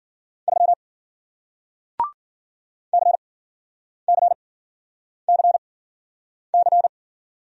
Morse code, heard as a keyed tone, about 53 wpm, spaced as for 9 wpm, 700 Hz. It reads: 3 CCCC